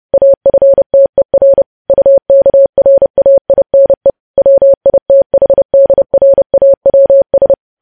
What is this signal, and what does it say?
Morse code, heard as a keyed tone, about 30 wpm, 550 Hz. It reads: AFTER UKRAINE WITHDRAWS